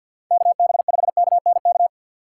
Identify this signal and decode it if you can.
Morse code, heard as a keyed tone, about 49 wpm, 700 Hz. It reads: KB5KNK
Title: USA Call Signs - 6 Characters - 6x Character Spacing - 50wpm